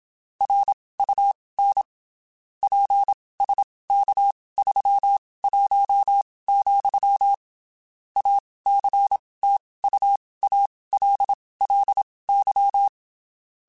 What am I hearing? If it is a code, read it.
Morse code, heard as a keyed tone, about 27 wpm, spaced as for 19 wpm, 800 Hz. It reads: RUN PSK31, ACTUALLY
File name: (spaced) / QSOs Spaced-mp3s / kc7sac @25 sp